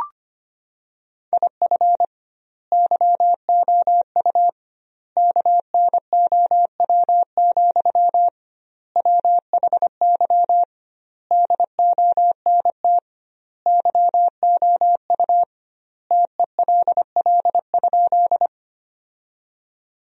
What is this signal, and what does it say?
Morse code, heard as a keyed tone, about 25 wpm, 700 Hz. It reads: IF YOU KNOW, WHY DONT YOU TELL?
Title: 2x Word-Spacing - Sentences from Top 300 Words - 25wpm